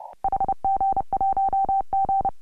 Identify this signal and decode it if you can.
Morse code, heard as a keyed tone, about 28 wpm, 750 Hz. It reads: HG1G
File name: AUDIO2018 / HG1G CW